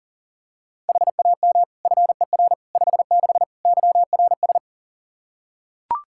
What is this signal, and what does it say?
Morse code, heard as a keyed tone, about 37 wpm, 700 Hz. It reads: HAM FER 56 YRS
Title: K1USN SST QSO Practice 40wpm